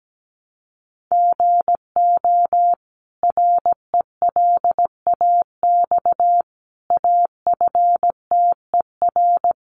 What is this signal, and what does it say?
Morse code, heard as a keyed tone, about 17 wpm, 700 Hz. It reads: GO RELAX AFTER
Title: Sentences from Top 1400 Words 17wpm